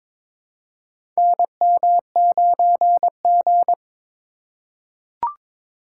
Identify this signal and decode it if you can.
Morse code, heard as a keyed tone, about 22 wpm, 700 Hz. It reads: NM9G
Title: USA Call Signs - 4 Characters - 3x Character Spacing - 22wpm